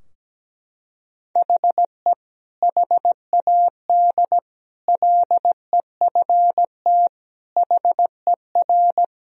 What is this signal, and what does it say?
Morse code, heard as a keyed tone, about 17 wpm, 700 Hz. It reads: HE HAD LEFT HER
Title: Sentences from Top 1000 Words - Review Part 3 - 17wpm